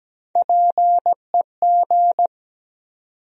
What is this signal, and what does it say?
Morse code, heard as a keyed tone, about 17 wpm, 700 Hz. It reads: PEG
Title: USA Common Names - Max 3 Letters - 17wpm